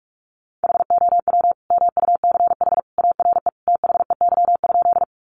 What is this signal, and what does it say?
Morse code, heard as a keyed tone, about 44 wpm, 700 Hz. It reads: HOW MUCH ARE THEY?
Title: Sentences from Top 200 Words 45wpm